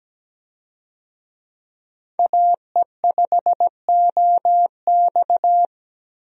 Morse code, heard as a keyed tone, about 17 wpm, 700 Hz. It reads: AE5OX
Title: USA Call Signs - 5 Characters - 17wpm